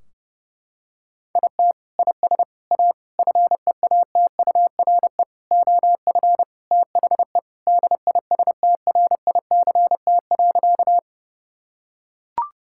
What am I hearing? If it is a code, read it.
Morse code, heard as a keyed tone, about 30 wpm, 700 Hz. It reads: IT IS A FEATURE OF THE DISTRICT.
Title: Sentences from Top 1600 Words 30wpm